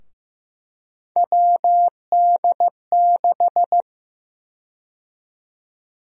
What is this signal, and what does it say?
Morse code, heard as a keyed tone, about 15 wpm, 700 Hz. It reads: WD6
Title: USA Call Sign Prefixes - Repeated 2x - 15wpm